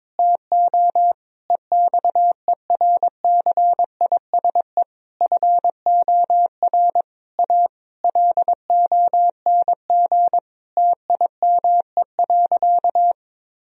Morse code, heard as a keyed tone, about 22 wpm, 700 Hz. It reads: TO EXERCISE FOR A LONG TIME.
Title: Sentences from Top 1000 Words - Review Part 4 - 22wpm